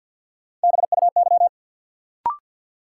Morse code, heard as a keyed tone, about 53 wpm, 700 Hz. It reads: BUY
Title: Sentences from Top 700 Words - New Word Drill - 50wpm